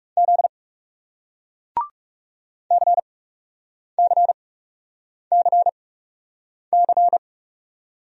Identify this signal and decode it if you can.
Morse code, heard as a keyed tone, about 40 wpm, spaced as for 8 wpm, 700 Hz. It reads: Z CCCC